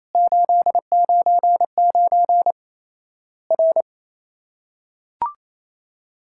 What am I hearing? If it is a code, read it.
Morse code, heard as a keyed tone, about 28 wpm, 700 Hz. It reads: 899 R
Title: Worked All Europe DX Contest - QTC Exchange - 28wpm